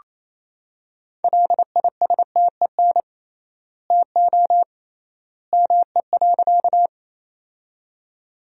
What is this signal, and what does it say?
Morse code, heard as a keyed tone, about 28 wpm, 700 Hz. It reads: LISTEN TO ME.
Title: 3x Word-Spacing - Sentences from Top 400 Words - 28wpm